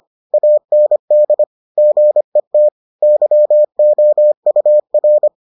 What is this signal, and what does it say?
Morse code, heard as a keyed tone, about 25 wpm, 600 Hz. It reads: AND GET YOUR